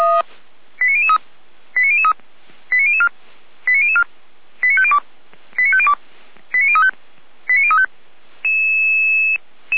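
Five-tone selective call (ZVEI), heard as quick runs of five tones, tones of about 70 ms. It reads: 89003 89003 89004 89004 89502 89502 89035 89035
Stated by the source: ZVEI five-tone call received on SDR, logged by Feuerwehr Orlamünde.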